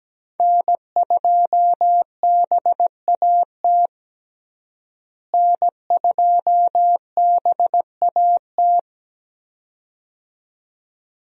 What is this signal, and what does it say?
Morse code, heard as a keyed tone, about 17 wpm, 700 Hz. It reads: N2BAT N2BAT
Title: USA Call Signs - 5 Characters - Repeated 2x - 17wpm